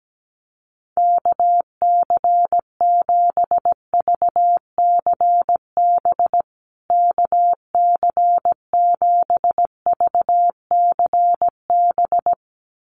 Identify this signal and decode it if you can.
Morse code, heard as a keyed tone, about 17 wpm, 700 Hz. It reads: KC7VCB KC7VCB
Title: USA Call Signs Repeated 2x 17wpm